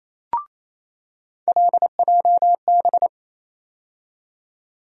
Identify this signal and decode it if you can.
Morse code, heard as a keyed tone, about 28 wpm, 700 Hz. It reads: LJB
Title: USA Call Sign Postfixes - 3 Letters 28wpm